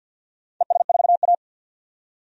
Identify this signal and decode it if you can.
Morse code, heard as a keyed tone, about 51 wpm, 700 Hz. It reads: ES4A
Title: DXCC-100 Call Signs - 4 Characters - 50wpm